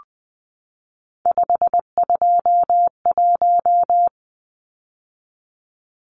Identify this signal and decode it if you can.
Morse code, heard as a keyed tone, about 20 wpm, 700 Hz. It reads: 521
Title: Numbers - 3-Digit 20wpm